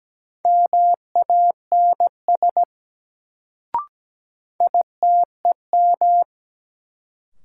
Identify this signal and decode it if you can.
Morse code, heard as a keyed tone, about 17 wpm, 700 Hz. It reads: MANS ITEM
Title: Sentences from Top 1000 Words - Review Part 6 - New Word Drill - 17wpm